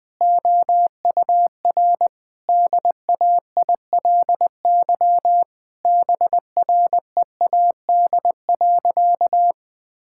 Morse code, heard as a keyed tone, about 20 wpm, 700 Hz. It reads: OUR DAILY BREAD.
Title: Sentences from Top 1600 Words 20wpm